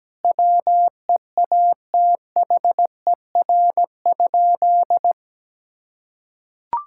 Morse code, heard as a keyed tone, about 17 wpm, 700 Hz. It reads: WEATHER?
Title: Sentences from Top 1000 Words - Review Part 5 - 17wpm